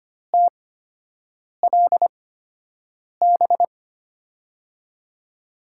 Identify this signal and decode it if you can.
Morse code, heard as a keyed tone, about 25 wpm, spaced as for 7 wpm, 700 Hz. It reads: TLB
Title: USA Call Signs - 5 Characters - 8x Character Spacing - 25wpm